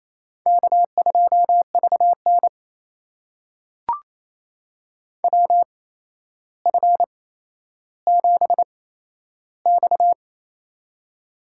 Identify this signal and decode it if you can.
Morse code, heard as a keyed tone, about 28 wpm, spaced as for 12 wpm, 700 Hz. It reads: K2VN WF7X
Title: USA Call Signs - 4 Characters - 8x Character Spacing - 28wpm